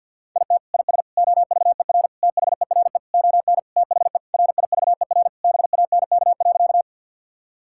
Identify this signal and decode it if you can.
Morse code, heard as a keyed tone, about 50 wpm, 700 Hz. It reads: IT IS OVER THERE ON THE RIVER BANK.
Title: Sentences from Top 1000 Words - Review Part 6 - 50wpm